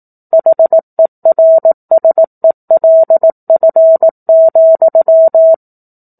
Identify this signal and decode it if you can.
Morse code, heard as a keyed tone, about 18 wpm, 650 Hz. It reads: HERSELF,
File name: 18 WPM